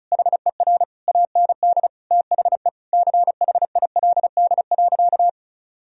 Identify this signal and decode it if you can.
Morse code, heard as a keyed tone, about 34 wpm, 700 Hz. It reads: HER AND THE CHILD.
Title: Sentences from Top 1000 Words - Review Part 1 - 35wpm